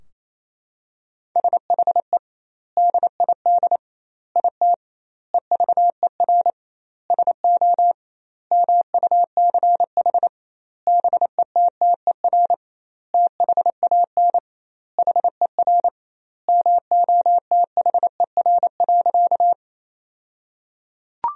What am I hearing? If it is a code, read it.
Morse code, heard as a keyed tone, about 28 wpm, 700 Hz. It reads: SHE DID IT EVER SO MUCH BETTER THAN HER MOTHER.
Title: 2x Word-Spacing - Sentences from Top 400 Words - 28wpm